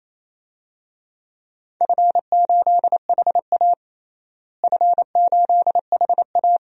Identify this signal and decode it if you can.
Morse code, heard as a keyed tone, about 28 wpm, 700 Hz. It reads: F8HA F8HA
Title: DXCC-100 Call Signs - 4 Characters - Repeated 2x - 28wpm